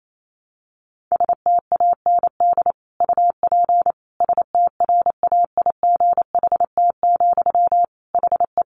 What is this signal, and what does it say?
Morse code, heard as a keyed tone, about 28 wpm, 700 Hz. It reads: STAND UP STRAIGHT, HE
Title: Sentences from Top 1000 Words 28wpm